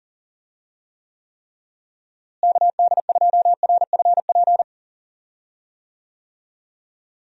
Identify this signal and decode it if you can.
Morse code, heard as a keyed tone, about 40 wpm, 700 Hz. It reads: KD2RFP
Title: USA Call Signs - 6 Characters - Repeated 3x - 40wpm